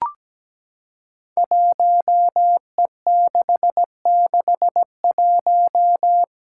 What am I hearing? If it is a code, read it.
Morse code, heard as a keyed tone, about 17 wpm, 700 Hz. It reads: 1E661
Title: Five Challenging Letter-Number Combinations 17wpm